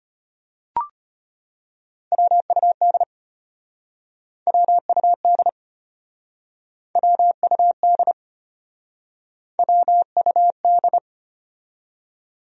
Morse code, heard as a keyed tone, about 33 wpm, 700 Hz. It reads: WUD WUD WUD WUD